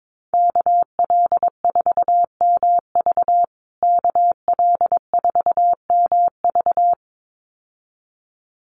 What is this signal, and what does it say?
Morse code, heard as a keyed tone, about 22 wpm, 700 Hz. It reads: KL4MV KL4MV
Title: USA Call Signs Repeated 2x 22wpm